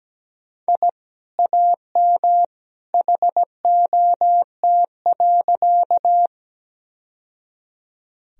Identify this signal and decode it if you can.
Morse code, heard as a keyed tone, about 17 wpm, 700 Hz. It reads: I AM HOT.
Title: Sentences from Top 1000 Words - Review Part 3 - 17wpm